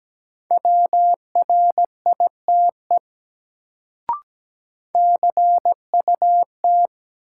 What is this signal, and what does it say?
Morse code, heard as a keyed tone, about 17 wpm, 700 Hz. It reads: WRITE CUT
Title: Sentences from Top 300 Words - New Word Drill - 17wpm